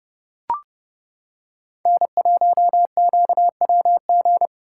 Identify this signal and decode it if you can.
Morse code, heard as a keyed tone, about 29 wpm, 700 Hz. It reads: N1QWG